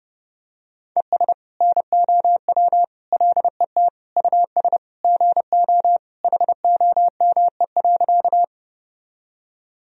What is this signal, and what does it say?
Morse code, heard as a keyed tone, about 30 wpm, 700 Hz. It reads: ES NOW LET US GO HOME.
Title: Sentences from Top 300 Words and Abbreviations 30wpm